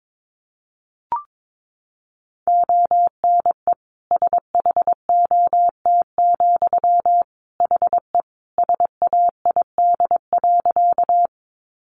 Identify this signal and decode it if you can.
Morse code, heard as a keyed tone, about 22 wpm, 700 Hz. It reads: ONE SHOT, HE SAID.
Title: Sentences from Top 1100 Words 22wpm